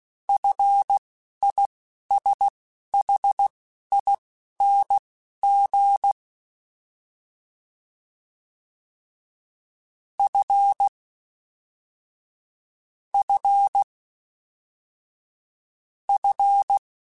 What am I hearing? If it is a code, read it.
Morse code, heard as a keyed tone, about 16 wpm, spaced as for 11 wpm, 800 Hz. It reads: FISHING F F F
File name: Fs / Single F mp3s / Single F Prefixes 3x2 Twice @10